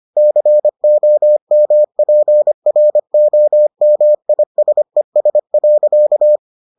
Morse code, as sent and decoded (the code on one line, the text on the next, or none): -.-. --- -- .--. .-. --- -- .. ... ... .-.-.-
COMPROMISES.